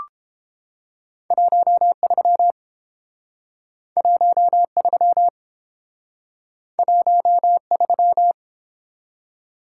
Morse code, heard as a keyed tone, about 29 wpm, 700 Hz. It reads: 13 13 13